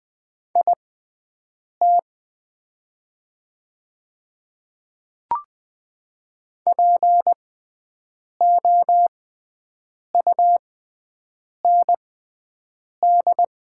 Morse code, heard as a keyed tone, about 20 wpm, spaced as for 7 wpm, 700 Hz. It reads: IT POUND
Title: 6x Character Spacing - Top 500 Words - 20wpm